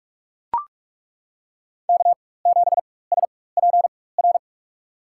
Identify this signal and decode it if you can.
Morse code, heard as a keyed tone, about 42 wpm, spaced as for 21 wpm, 700 Hz. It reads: K7SPR
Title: USA Call Signs - 5 Characters - 4x Character Spacing - 45wpm